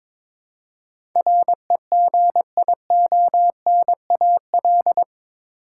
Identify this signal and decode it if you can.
Morse code, heard as a keyed tone, about 22 wpm, 700 Hz. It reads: REGIONAL
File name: practice / 22wpm / Eight Letter Words - 22wpm